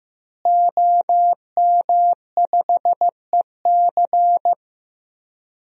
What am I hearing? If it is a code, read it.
Morse code, heard as a keyed tone, about 16 wpm, 700 Hz. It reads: OM5EC
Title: DXCC-100 Call Signs - 5 Characters - Repeated 3x - 15wpm